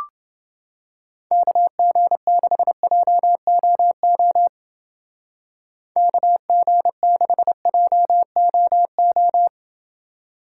Morse code, heard as a keyed tone, about 29 wpm, 700 Hz. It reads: KG6JOO KG6JOO